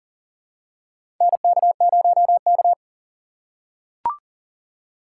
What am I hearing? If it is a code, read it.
Morse code, heard as a keyed tone, about 40 wpm, 700 Hz. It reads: NK0K
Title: USA Call Signs - 2x Character Spacing - 40wpm